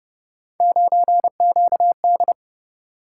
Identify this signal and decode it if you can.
Morse code, heard as a keyed tone, about 31 wpm, 700 Hz. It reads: 9QD
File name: Speed-Racing - Three Letter-Number Combinations - 25wpm